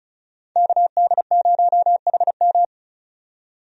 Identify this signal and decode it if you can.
Morse code, heard as a keyed tone, about 36 wpm, 700 Hz. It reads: KD0HM
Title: USA Call Signs - 5 Characters - 35wpm